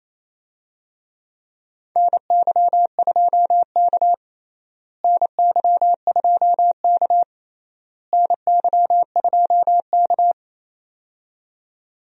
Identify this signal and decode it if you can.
Morse code, heard as a keyed tone, about 28 wpm, 700 Hz. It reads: NY2K NY2K NY2K
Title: USA Call Signs - 4 Characters - Repeated 3x - 28wpm